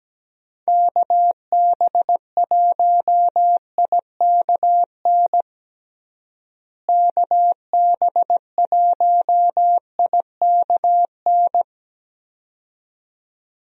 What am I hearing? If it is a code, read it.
Morse code, heard as a keyed tone, about 17 wpm, 700 Hz. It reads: KB1IKN KB1IKN
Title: USA Call Signs - 6 Characters - Repeated 2x - 17wpm